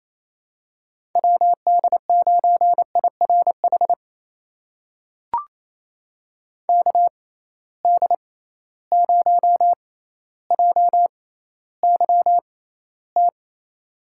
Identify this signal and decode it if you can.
Morse code, heard as a keyed tone, about 27 wpm, spaced as for 14 wpm, 700 Hz. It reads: WD9IRH KD0JYT
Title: USA Call Signs - 6 Characters - 6x Character Spacing - 28wpm